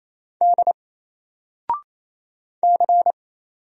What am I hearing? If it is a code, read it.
Morse code, heard as a keyed tone, about 28 wpm, 700 Hz. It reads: D C
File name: Single-Letter-Number-28wpm